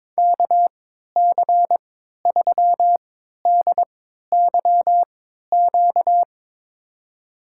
Morse code, heard as a keyed tone, about 22 wpm, spaced as for 12 wpm, 700 Hz. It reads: KC3DYQ
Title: USA Call Signs - 3x Character Spacing - 22wpm